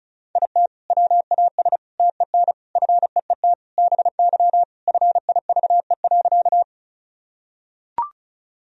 Morse code, heard as a keyed tone, about 36 wpm, 700 Hz. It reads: IT WAS TEN FEET BY FIVE.
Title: Sentences from Top 1000 Words - Review Part 2 - 35wpm